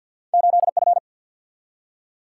Morse code, heard as a keyed tone, about 52 wpm, 700 Hz. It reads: 8 F
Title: Single-Letter Single-Number 50wpm